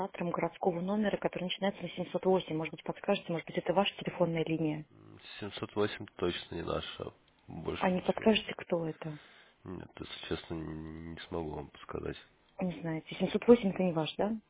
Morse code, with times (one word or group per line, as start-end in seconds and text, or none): none